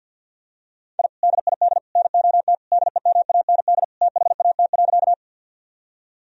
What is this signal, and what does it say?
Morse code, heard as a keyed tone, about 51 wpm, 700 Hz. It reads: I DID NOT DEMAND THAT.
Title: Sentences from Top 1000 Words - Review Part 4 - 50wpm